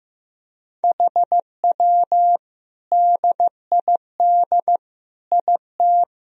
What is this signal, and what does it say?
Morse code, heard as a keyed tone, about 15 wpm, 700 Hz. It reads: HW DID IT
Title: Sentences from Top 300 Words and Abbreviations 15wpm